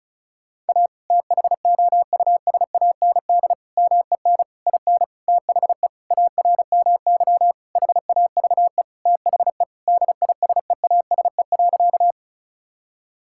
Morse code, heard as a keyed tone, about 36 wpm, 700 Hz. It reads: A THOUSAND MEN IN THE ARMY HAVE THE DISEASE.
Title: Sentences from Top 1000 Words - Review Part 6 - 35wpm